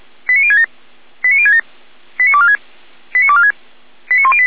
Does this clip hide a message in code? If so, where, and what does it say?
Yes: five-tone selective call (ZVEI) from 0.3 to 4.5 s, heard as quick runs of five tones, tones of about 70 ms. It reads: 89067 89067 89246 89246 89108